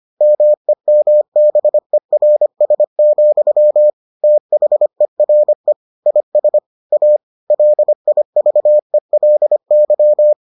Morse code, heard as a keyed tone, about 25 wpm, 600 Hz. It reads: MEMBERS, THERE IS A LIVELY